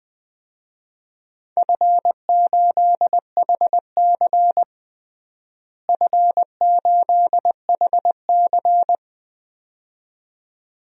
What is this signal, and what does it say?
Morse code, heard as a keyed tone, about 21 wpm, 700 Hz. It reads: F8HC F8HC
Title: DXCC-100 Call Signs - 4 Characters - Repeated 2x - 20wpm